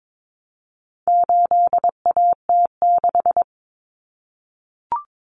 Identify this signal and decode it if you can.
Morse code, heard as a keyed tone, about 22 wpm, 700 Hz. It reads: 8AT6
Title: Common Cut Numbers - 4-Digit 22wpm